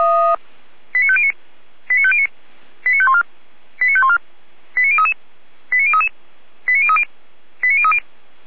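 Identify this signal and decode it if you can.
Five-tone selective call (ZVEI), heard as quick runs of five tones, tones of about 70 ms. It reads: 89509 89509 89514 89514 89033 89033 89030 89030
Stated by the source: ZVEI five-tone selective call 89011 received with an SDR.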